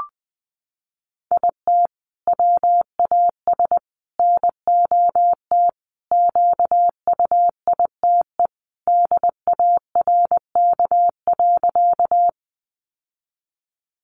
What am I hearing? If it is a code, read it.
Morse code, heard as a keyed tone, about 20 wpm, 700 Hz. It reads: IT WAS NOT QUITE DARK.
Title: Sentences from Top 700 Words 20wpm